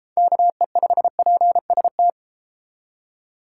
Morse code, heard as a keyed tone, about 31 wpm, 700 Hz. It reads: KE5PST